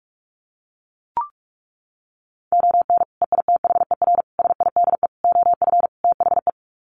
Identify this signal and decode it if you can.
Morse code, heard as a keyed tone, about 44 wpm, 700 Hz. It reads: ON EITHER SIDE OF THE
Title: Sets of 5 Words 45wpm